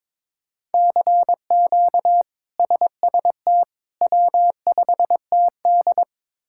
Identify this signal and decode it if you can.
Morse code, heard as a keyed tone, about 22 wpm, 700 Hz. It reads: CQ SST W5TD
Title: K1USN SST QSO Practice - 22wpm